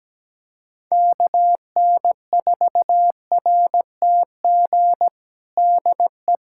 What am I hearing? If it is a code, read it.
Morse code, heard as a keyed tone, about 17 wpm, 700 Hz. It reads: KN4RTG DE
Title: USA Callsign DE Callsign - 6 Characters - 17wpm